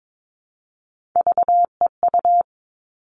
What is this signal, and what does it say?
Morse code, heard as a keyed tone, about 22 wpm, 700 Hz. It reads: VEU
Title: All Cut Numbers - 3-Digit 22wpm